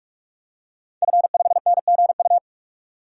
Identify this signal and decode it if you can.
Morse code, heard as a keyed tone, about 48 wpm, 700 Hz. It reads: F5NGU
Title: DXCC-100 Call Signs - 5 Characters - 45wpm